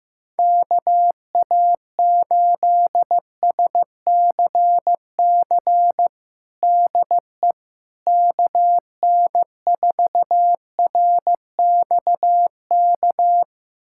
Morse code, heard as a keyed tone, about 15 wpm, 700 Hz. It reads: KA8SCC DE KN4RXK
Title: USA Callsign DE Callsign 15wpm